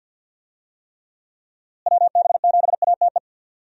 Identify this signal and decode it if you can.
Morse code, heard as a keyed tone, about 48 wpm, 700 Hz. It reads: WB7ATE
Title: USA Call Signs - 6 Characters - Repeated 2x - 50wpm